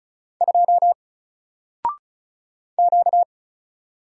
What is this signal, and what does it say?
Morse code, heard as a keyed tone, about 36 wpm, 700 Hz. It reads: 2 Q
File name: Single-Letter-Number-35wpm